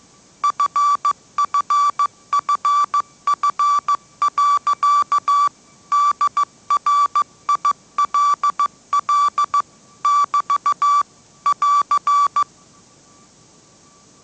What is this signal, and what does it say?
Morse code, heard as a keyed tone, about 15 wpm, 1160 Hz. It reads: FFFF. DRILL <BT> <AR>